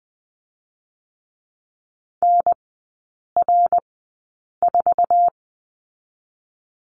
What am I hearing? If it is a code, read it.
Morse code, heard as a keyed tone, about 20 wpm, spaced as for 8 wpm, 700 Hz. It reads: NR4
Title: 067 Lesson 12e - Letter d - Three Characters 20wpm